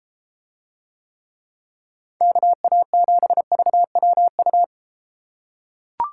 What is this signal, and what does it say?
Morse code, heard as a keyed tone, about 32 wpm, 700 Hz. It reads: KA7VWU